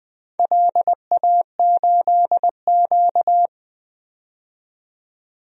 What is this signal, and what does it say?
Morse code, heard as a keyed tone, about 20 wpm, 700 Hz. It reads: LA8Q
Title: DXCC-100 Call Signs - 4 Characters - 20wpm